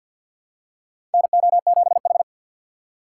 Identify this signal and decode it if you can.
Morse code, heard as a keyed tone, about 53 wpm, 700 Hz. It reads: NO7H
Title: USA Call Signs - 2x Character Spacing - 50wpm